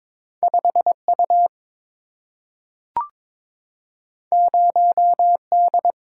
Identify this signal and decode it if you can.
Morse code, heard as a keyed tone, about 22 wpm, 700 Hz. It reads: 5U 0D
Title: Single-Letter Single-Number 22wpm